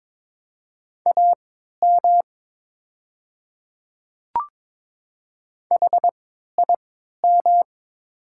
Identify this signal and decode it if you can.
Morse code, heard as a keyed tone, about 22 wpm, spaced as for 13 wpm, 700 Hz. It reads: AM HIM
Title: 3x Character Spacing - Top 500 Words - 22wpm